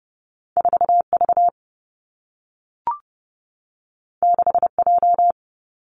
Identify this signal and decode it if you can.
Morse code, heard as a keyed tone, about 30 wpm, 700 Hz. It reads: 4V 6J
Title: Two Challenging Letter-Number Combinations 30wpm